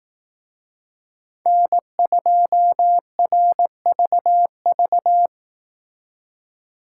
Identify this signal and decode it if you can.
Morse code, heard as a keyed tone, about 18 wpm, 700 Hz. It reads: N2RVV
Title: Speed Racing - USA 1x3 Call Signs 15wpm